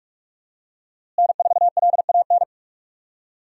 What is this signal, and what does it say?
Morse code, heard as a keyed tone, about 46 wpm, 700 Hz. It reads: N4LAN
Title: USA Call Signs - 5 Characters - Repeated 2x - 45wpm